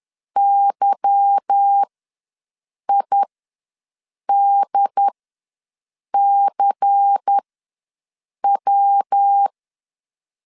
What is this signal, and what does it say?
Morse code, heard as a keyed tone, about 10 wpm, spaced as for 6 wpm, 800 Hz. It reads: YIDCW
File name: LEZIONI / 9 286 ws 10wpm 4032